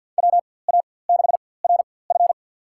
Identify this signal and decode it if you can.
Morse code, heard as a keyed tone, about 52 wpm, spaced as for 24 wpm, 700 Hz. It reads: WA6RF
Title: USA Call Signs - 5 Characters - 4x Character Spacing - 50wpm